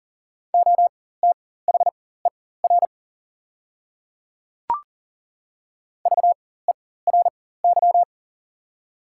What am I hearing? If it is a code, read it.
Morse code, heard as a keyed tone, about 40 wpm, spaced as for 19 wpm, 700 Hz. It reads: OTHER VERY
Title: 4x Character Spacing - Top 200 Words - 40wpm